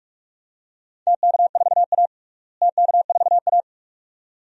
Morse code, heard as a keyed tone, about 46 wpm, 700 Hz. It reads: TK4A TK4A
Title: DXCC-100 Call Signs - 4 Characters - Repeated 2x - 45wpm